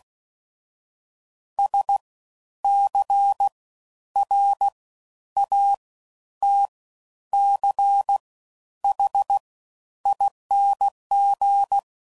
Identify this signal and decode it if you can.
Morse code, heard as a keyed tone, about 16 wpm, spaced as for 10 wpm, 800 Hz. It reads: SCRATCHING
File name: Suffixes / Special INGs / Fast Ings